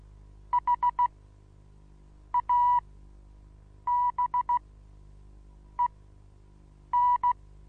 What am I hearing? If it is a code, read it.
Morse code, heard as a keyed tone, about 15 wpm, spaced as for 6 wpm, 980 Hz. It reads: HABEN
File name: Onlinelehrgang / morsen / DJ4UF Morsen25